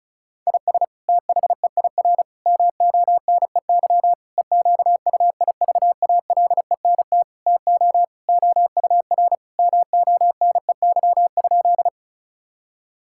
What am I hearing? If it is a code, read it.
Morse code, heard as a keyed tone, about 34 wpm, 700 Hz. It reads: IS THEIR MONEY EQUIVALENT TO OUR MONEY?
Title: Sentences from Top 1600 Words 35wpm